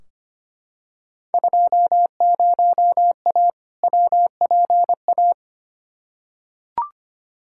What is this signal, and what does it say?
Morse code, heard as a keyed tone, about 25 wpm, 700 Hz. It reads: 20A WPA
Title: Field Day Exchanges Encore - 25wpm